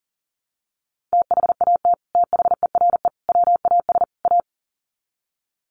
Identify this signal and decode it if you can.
Morse code, heard as a keyed tone, about 40 wpm, 700 Hz. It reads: THAT THERE WAS A
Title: Sets of 4 Words 40wpm